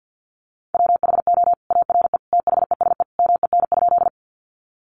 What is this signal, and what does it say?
Morse code, heard as a keyed tone, about 48 wpm, 700 Hz. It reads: WHO ARE THESE MEN?
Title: Sentences from Top 200 Words 50wpm